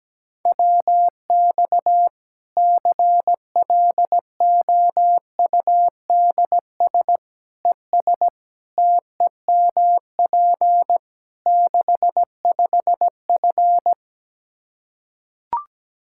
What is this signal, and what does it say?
Morse code, heard as a keyed tone, about 17 wpm, 700 Hz. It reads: WX CLOUDS ES TEMP 65F
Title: Practice QSO - Exchange 2 - Info - 17wpm